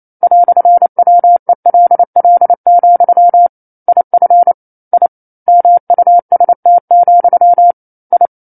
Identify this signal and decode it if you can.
Morse code, heard as a keyed tone, about 28 wpm, 700 Hz. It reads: "WELL, IF I MUST, I
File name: 09-Chapter-09 28 WPM